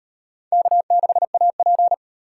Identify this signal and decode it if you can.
Morse code, heard as a keyed tone, about 35 wpm, 700 Hz. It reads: K6AP